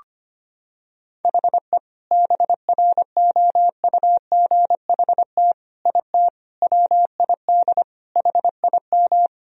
Morse code, heard as a keyed tone, about 25 wpm, 700 Hz. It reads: HE BROUGHT IT WID HIM
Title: Sentences from Top 500 Words and Abbreviations 25wpm